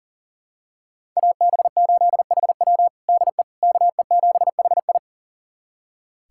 Morse code, heard as a keyed tone, about 40 wpm, 700 Hz. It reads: AB8HW DE KE7HI
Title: USA Callsign DE Callsign - 5 Characters - 40wpm